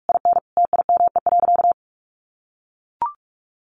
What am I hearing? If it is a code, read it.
Morse code, heard as a keyed tone, about 44 wpm, 700 Hz. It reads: IN TIME.
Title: Sentences from Top 1400 Words 45wpm